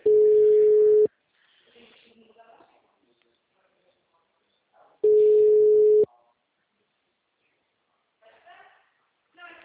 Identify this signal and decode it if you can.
Ringback tone (Europe, 425 Hz), with speech.
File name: out- -2005- - - .78